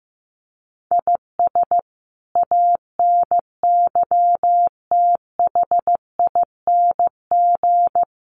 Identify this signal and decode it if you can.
Morse code, heard as a keyed tone, about 15 wpm, 700 Hz. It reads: IS ANYTHING
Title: Sentences from Top 1000 Words 15wpm